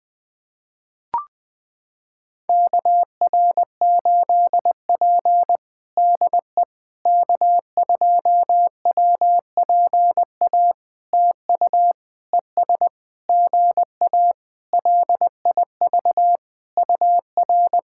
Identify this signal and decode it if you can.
Morse code, heard as a keyed tone, about 20 wpm, 700 Hz. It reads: KR8P DE K2WPA TU ES GA LIV UR
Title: POTA QSO Practice - 20wpm